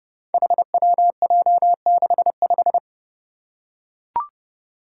Morse code, heard as a keyed tone, about 32 wpm, 700 Hz. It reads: HWJ65
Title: Five Challenging Letter-Number Combinations 30wpm